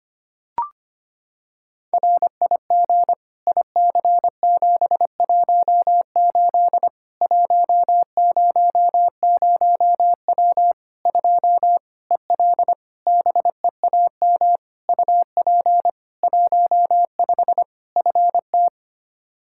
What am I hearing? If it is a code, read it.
Morse code, heard as a keyed tone, about 25 wpm, 700 Hz. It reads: RIG IC718 100W 2 EL BEAM UP 15 FT
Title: Practice QSO - Exchange 2 - Info - 25wpm